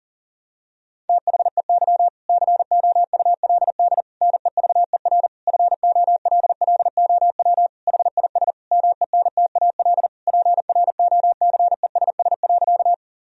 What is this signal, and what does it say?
Morse code, heard as a keyed tone, about 40 wpm, 690 Hz. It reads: THEY COULD NEVER FOLLOW HIS MENTAL PROCESS.
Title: Sentences from Top 1000 Words - Review Part 4 - 40wpm